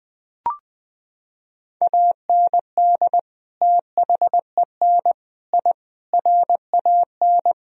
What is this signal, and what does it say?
Morse code, heard as a keyed tone, about 20 wpm, 700 Hz. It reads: AND THEN I RAN
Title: Sentences from Top 1000 Words - Review Part 3 - 20wpm